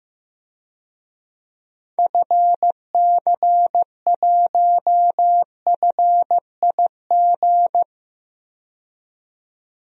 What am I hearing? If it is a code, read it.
Morse code, heard as a keyed tone, about 15 wpm, 700 Hz. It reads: FC1FIG
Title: DXCC-100 Call Signs - 6 Characters - 15wpm